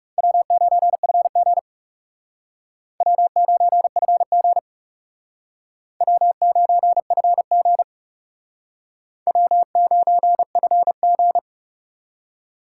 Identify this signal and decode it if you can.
Morse code, heard as a keyed tone, about 37 wpm, 700 Hz. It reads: W9FG W9FG W9FG W9FG
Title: Speed Racing - USA 1x2 Call Signs 30wpm